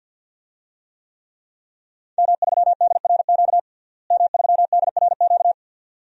Morse code, heard as a keyed tone, about 48 wpm, 700 Hz. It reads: M3DRQ M3DRQ
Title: DXCC-100 Call Signs - 5 Characters - Repeated 2x - 50wpm